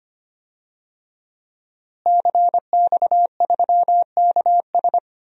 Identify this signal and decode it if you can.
Morse code, heard as a keyed tone, about 25 wpm, 700 Hz. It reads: CX3KS
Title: DXCC-100 Call Signs - 5 Characters - Repeated 2x - 25wpm